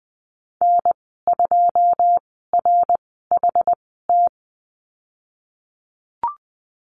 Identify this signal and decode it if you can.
Morse code, heard as a keyed tone, about 20 wpm, spaced as for 15 wpm, 700 Hz. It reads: N2RHT
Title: 128 Lesson 21g - Number 6 - Call Signs 20wpm